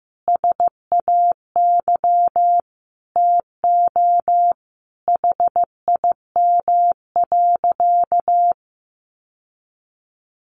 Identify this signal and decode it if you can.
Morse code, heard as a keyed tone, about 15 wpm, 700 Hz. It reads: SAY TO HIM.
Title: Sentences from Top 1000 Words 15wpm